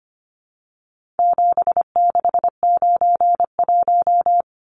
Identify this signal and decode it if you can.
Morse code, heard as a keyed tone, about 25 wpm, 700 Hz. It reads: 7691
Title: Numbers - 4-Digit 25wpm